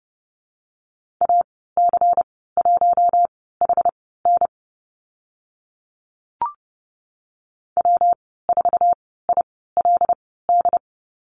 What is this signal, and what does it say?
Morse code, heard as a keyed tone, about 30 wpm, spaced as for 17 wpm, 700 Hz. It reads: AC1HN W4ILD
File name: Lesson 19g - Letter w - Call Signs 30wpm